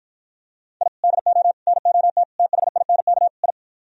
Morse code, heard as a keyed tone, about 52 wpm, 700 Hz. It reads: I DO NOT THINK I